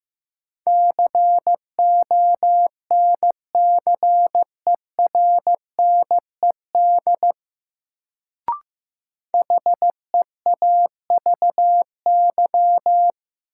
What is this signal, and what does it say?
Morse code, heard as a keyed tone, about 15 wpm, 700 Hz. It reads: CONCERNED HEAVY
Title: Sentences from Top 1000 Words - Review Part 6 - New Word Drill - 15wpm